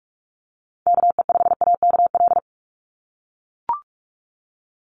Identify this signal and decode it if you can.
Morse code, heard as a keyed tone, about 44 wpm, 700 Hz. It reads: KE5AKL
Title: NA SOTA Chaser Callsigns 45wpm